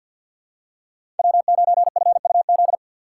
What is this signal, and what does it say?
Morse code, heard as a keyed tone, about 53 wpm, 700 Hz. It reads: W9FUZ
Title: USA Call Signs - 5 Characters - 3x Character Spacing - 50wpm